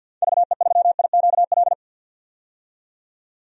Morse code, heard as a keyed tone, about 47 wpm, 700 Hz. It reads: VE3IQL
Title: DXCC-100 Call Signs - 6 Characters - 50wpm